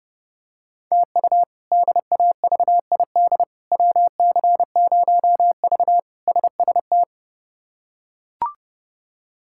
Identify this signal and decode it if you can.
Morse code, heard as a keyed tone, about 32 wpm, 700 Hz. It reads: TU DAVID WC0V SST
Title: K1USN SST QSO Practice - 30wpm